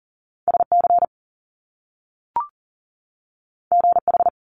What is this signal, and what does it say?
Morse code, heard as a keyed tone, about 42 wpm, 700 Hz. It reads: SC GH
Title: Top Two Letter Combinations in Words 40wpm